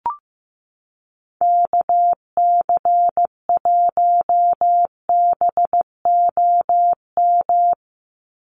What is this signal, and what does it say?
Morse code, heard as a keyed tone, about 15 wpm, 700 Hz. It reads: KC1BOM
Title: USA Call Signs 15wpm